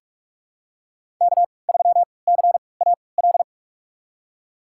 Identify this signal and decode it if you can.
Morse code, heard as a keyed tone, about 46 wpm, spaced as for 25 wpm, 700 Hz. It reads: K3CAL
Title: USA Call Signs - 5 Characters - 3x Character Spacing - 45wpm